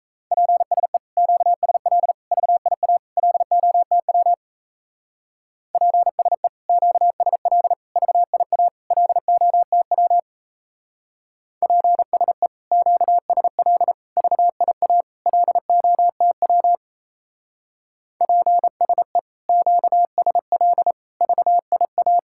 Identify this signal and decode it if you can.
Morse code, heard as a keyed tone, about 35 wpm, 700 Hz. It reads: PSE QSL VIA LOTW PSE QSL VIA LOTW PSE QSL VIA LOTW PSE QSL VIA
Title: Speed-Racing Common QSO Phrases - 28wpm